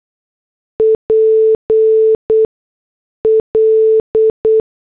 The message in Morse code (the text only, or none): PL